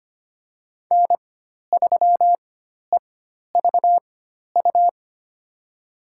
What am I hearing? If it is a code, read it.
Morse code, heard as a keyed tone, about 25 wpm, spaced as for 12 wpm, 700 Hz. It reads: N3EVU
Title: USA Call Signs - 5 Characters - 4x Character Spacing - 25wpm